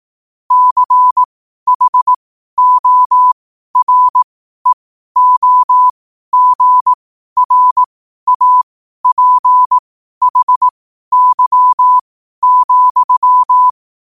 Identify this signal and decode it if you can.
Morse code, heard as a keyed tone, about 18 wpm, spaced as for 12 wpm, 1000 Hz. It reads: CHOREOGRAPHY,
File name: qotdfiles / 12wpm